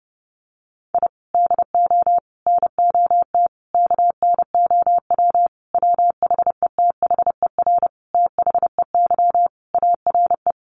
Morse code, heard as a keyed tone, about 30 wpm, 700 Hz. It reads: I DO NOT KNOW WHETHER THEY ARE
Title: Sentences from Top 1500 Words 30wpm